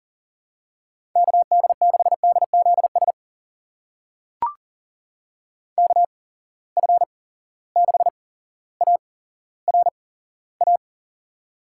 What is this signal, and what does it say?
Morse code, heard as a keyed tone, about 40 wpm, spaced as for 17 wpm, 700 Hz. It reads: KD6DZS KF6ARA
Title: USA Call Signs - 6 Characters - 8x Character Spacing - 40wpm